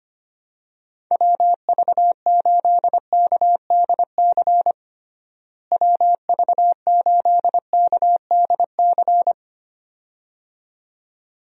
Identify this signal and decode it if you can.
Morse code, heard as a keyed tone, about 25 wpm, 700 Hz. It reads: WV8KDC WV8KDC
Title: USA Call Signs - 6 Characters - Repeated 2x - 25wpm